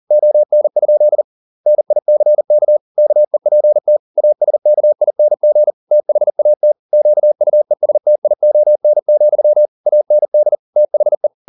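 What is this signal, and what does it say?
Morse code, heard as a keyed tone, about 40 wpm, 600 Hz. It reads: ON? NICK KEPT ASKING THAT QUESTION, AND THE
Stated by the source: Morse code at 40 wpm, practice text, CW med Gnister.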